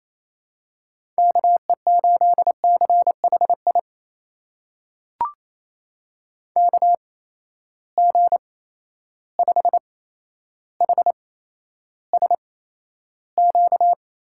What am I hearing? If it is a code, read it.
Morse code, heard as a keyed tone, about 28 wpm, spaced as for 12 wpm, 700 Hz. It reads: KE8CHI KG5HSQ
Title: USA Call Signs - 6 Characters - 8x Character Spacing - 28wpm